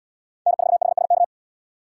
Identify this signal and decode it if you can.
Morse code, heard as a keyed tone, about 73 wpm, 700 Hz. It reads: N4HSK